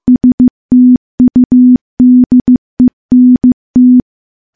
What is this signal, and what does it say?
Morse code, heard as a keyed tone, about 15 wpm, 250 Hz. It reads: STUDENT
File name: morse / morse-training-15WPM